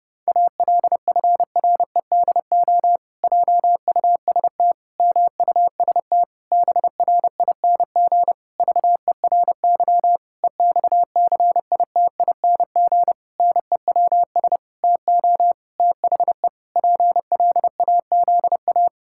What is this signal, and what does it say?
Morse code, heard as a keyed tone, about 32 wpm, 700 Hz. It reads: ALFREDO JUST MUST BRING VERY EXCITING NEWS TO THE PLAZA
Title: Pangram Challenge v2 - 30wpm